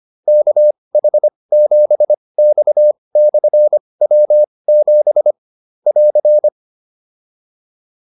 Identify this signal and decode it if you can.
Morse code, heard as a keyed tone, about 25 wpm, spaced as for 20 wpm, 600 Hz. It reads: KH7X/W7 <AR>
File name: QSO104 20